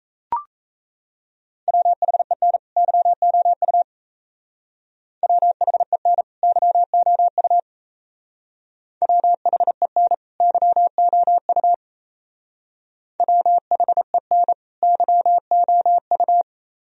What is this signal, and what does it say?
Morse code, heard as a keyed tone, about 35 wpm, 700 Hz. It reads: WHEN YOU WHEN YOU WHEN YOU WHEN YOU